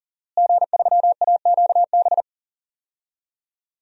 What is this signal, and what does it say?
Morse code, heard as a keyed tone, about 40 wpm, 700 Hz. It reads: G3AQB
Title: DXCC-100 Call Signs - 5 Characters - Repeated 2x - 40wpm